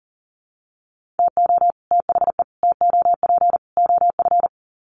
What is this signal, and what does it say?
Morse code, heard as a keyed tone, about 40 wpm, 700 Hz. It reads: TO THE TOP OF
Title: Sets of 4 Words 40wpm